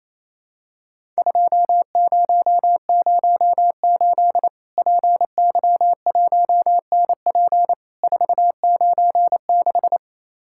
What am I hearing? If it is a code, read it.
Morse code, heard as a keyed tone, about 28 wpm, 700 Hz. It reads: 2008 PY1NP 496
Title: Worked All Europe DX Contest - QTC Exchange - 28wpm